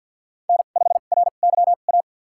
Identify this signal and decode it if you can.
Morse code, heard as a keyed tone, about 50 wpm, spaced as for 36 wpm, 700 Hz. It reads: N5RYA
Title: USA Call Signs - 5 Characters - 2x Character Spacing - 50wpm